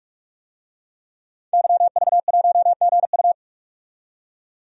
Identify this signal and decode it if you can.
Morse code, heard as a keyed tone, about 48 wpm, 700 Hz. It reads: YV1GU
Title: DXCC-100 Call Signs - 5 Characters - 45wpm